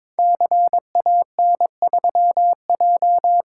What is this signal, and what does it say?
Morse code, heard as a keyed tone, about 22 wpm, 700 Hz. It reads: CAN3J
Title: Code Groups - 5 Letter:Numbers - 22wpm